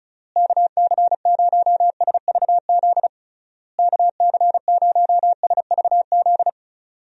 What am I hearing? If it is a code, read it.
Morse code, heard as a keyed tone, about 36 wpm, 700 Hz. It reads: KC0SVZ KC0SVZ
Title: USA Call Signs - 6 Characters - Repeated 3x - 35wpm